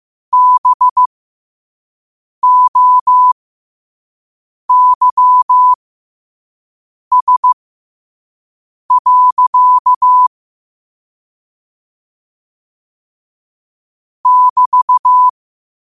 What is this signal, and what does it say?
Morse code, heard as a keyed tone, about 15 wpm, spaced as for 5 wpm, 1000 Hz. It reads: BOYS. <BT>
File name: qotdfiles / 5wpm